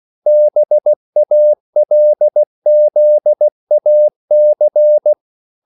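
Morse code, heard as a keyed tone, about 16 wpm, 600 Hz. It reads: BALZAC
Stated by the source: English quotations sent in CW as practice at 16 words per minute.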